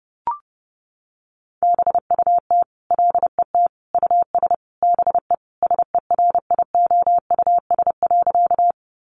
Morse code, heard as a keyed tone, about 30 wpm, 700 Hz. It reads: BUT LET US BE SERIOUS.
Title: Sentences from Top 1100 Words 30wpm